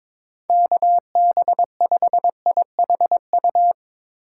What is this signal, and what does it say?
Morse code, heard as a keyed tone, about 24 wpm, 700 Hz. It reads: KB5IHU
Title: Speed Racing - USA Call Signs 22wpm